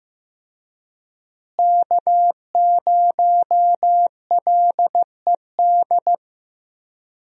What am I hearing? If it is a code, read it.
Morse code, heard as a keyed tone, about 15 wpm, 700 Hz. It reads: K0LED
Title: USA Call Signs - 5 Characters - Repeated 2x - 15wpm